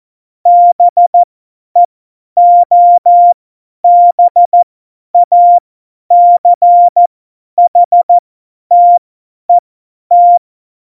Morse code, heard as a keyed tone, about 14 wpm, spaced as for 10 wpm, 700 Hz. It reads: BEOBACHTET
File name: Morsekurs / Witze / Witz 02. Oktober 2018, 50 F70